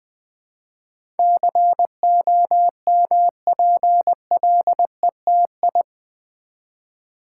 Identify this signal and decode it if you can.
Morse code, heard as a keyed tone, about 20 wpm, 700 Hz. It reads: COMPLETI
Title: Build A Word - 4 Letters At A Time - 10-Letter Words - 20wpm